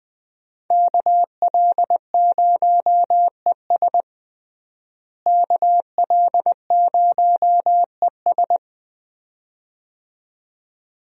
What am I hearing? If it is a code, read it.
Morse code, heard as a keyed tone, about 21 wpm, 700 Hz. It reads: KL0ES KL0ES
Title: USA Call Signs - 5 Characters - Repeated 2x - 20wpm